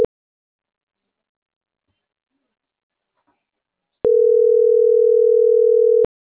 Ringback tone (North America), 440 + 480 Hz.
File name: CENTRAL MAYAB / out atzhyry.gomez